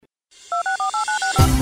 DTMF, heard as telephone touch tones, tones of about 120 ms. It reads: 264896